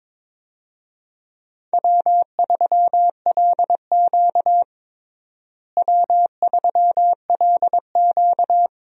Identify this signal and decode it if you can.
Morse code, heard as a keyed tone, about 22 wpm, 700 Hz. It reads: W3LQ W3LQ
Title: USA Call Signs - 4 Characters - Repeated 2x - 22wpm